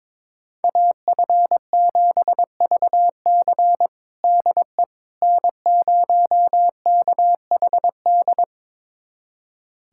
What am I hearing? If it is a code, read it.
Morse code, heard as a keyed tone, about 22 wpm, 700 Hz. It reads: AF7VC DE N0KHD
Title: USA Callsign DE Callsign - 5 Characters - 22wpm